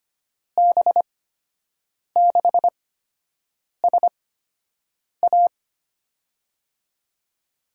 Morse code, heard as a keyed tone, about 25 wpm, spaced as for 7 wpm, 700 Hz. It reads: B6SA
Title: USA Call Signs - 8x Character Spacing - 25wpm